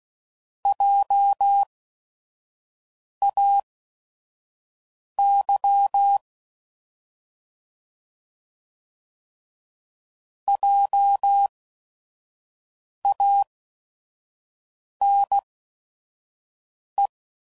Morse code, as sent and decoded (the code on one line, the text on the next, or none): .--- .- -.-- / .--- .- -. .
JAY JANE